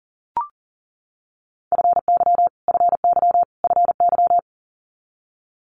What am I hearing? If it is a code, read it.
Morse code, heard as a keyed tone, about 41 wpm, 700 Hz. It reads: FY FY FY
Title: Multi-letter Phonemes Repeated 3x 40wpm